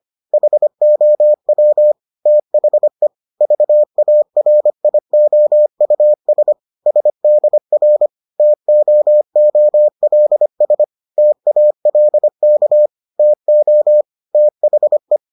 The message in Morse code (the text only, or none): HOW THE VARIOUS SDR TOOLS TALK TO THE